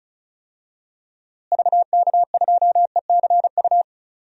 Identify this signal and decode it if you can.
Morse code, heard as a keyed tone, about 36 wpm, 700 Hz. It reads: VK2ECU
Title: DXCC-100 Call Signs - 6 Characters - Repeated 3x - 35wpm